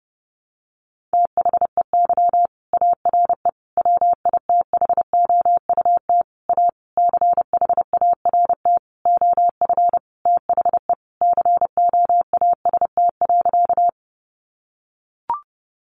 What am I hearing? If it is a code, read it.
Morse code, heard as a keyed tone, about 30 wpm, 700 Hz. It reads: THEY ARE WITHOUT A CHART OF THE COAST.
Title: Sentences from Top 1300 Words 30wpm